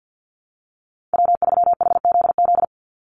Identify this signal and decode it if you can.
Morse code, heard as a keyed tone, about 48 wpm, 700 Hz. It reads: W3HZZ
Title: Call Signs Repeated 3x - 50wpm